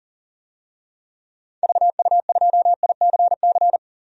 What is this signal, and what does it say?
Morse code, heard as a keyed tone, about 40 wpm, 700 Hz. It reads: VU2ICC
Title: DXCC-100 Call Signs - 6 Characters - 40wpm